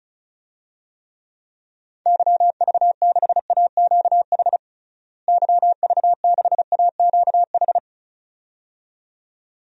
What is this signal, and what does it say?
Morse code, heard as a keyed tone, about 34 wpm, 700 Hz. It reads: YV6AQH YV6AQH
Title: DXCC-100 Call Signs - 6 Characters - Repeated 2x - 35wpm